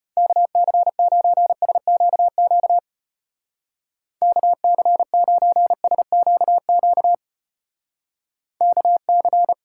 Morse code, heard as a keyed tone, about 36 wpm, 700 Hz. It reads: KC9SQQ KC9SQQ KC